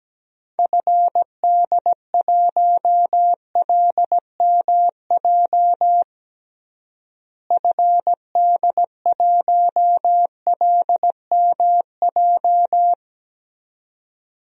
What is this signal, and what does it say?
Morse code, heard as a keyed tone, about 17 wpm, 700 Hz. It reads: FD1LMJ FD1LMJ
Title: DXCC-100 Call Signs - 6 Characters - Repeated 2x - 17wpm